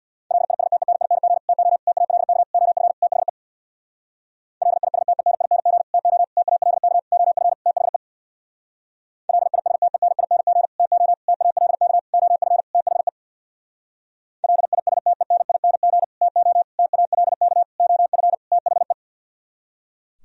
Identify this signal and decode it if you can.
Morse code, heard as a keyed tone, about 64 wpm, 690 Hz. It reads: LISTENING TO TALK OF THE LISTENING TO TALK OF THE LISTENING TO TALK OF THE LISTENING TO TALK OF THE